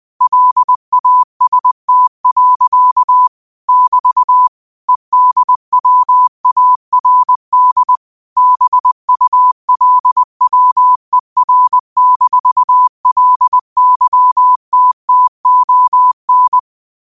Morse code, heard as a keyed tone, about 20 wpm, 1000 Hz. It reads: LAST. <BT> EDWARD BULWER-LYTTON
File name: qotdfiles / 20wpm